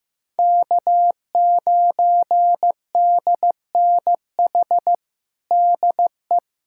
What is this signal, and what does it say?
Morse code, heard as a keyed tone, about 15 wpm, 700 Hz. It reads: K9DNH DE
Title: USA Callsign DE Callsign - 5 Characters - 15wpm